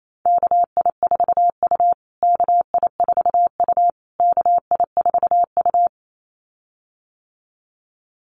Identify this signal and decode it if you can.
Morse code, heard as a keyed tone, about 28 wpm, 700 Hz. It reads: KI4U KI4U KI4U
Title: USA Call Signs Repeated 3x 28wpm